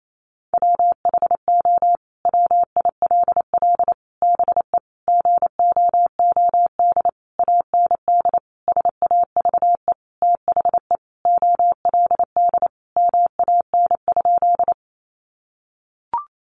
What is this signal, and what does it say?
Morse code, heard as a keyed tone, about 28 wpm, 700 Hz. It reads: WHO WILL BE GOOD AND SAVE THE OLD MAN?
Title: Sentences from Top 1100 Words 28wpm